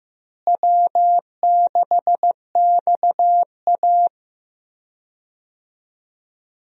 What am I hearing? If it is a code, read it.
Morse code, heard as a keyed tone, about 15 wpm, 700 Hz. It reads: W6XA
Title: USA Call Signs - 4 Characters - Repeated 3x - 15wpm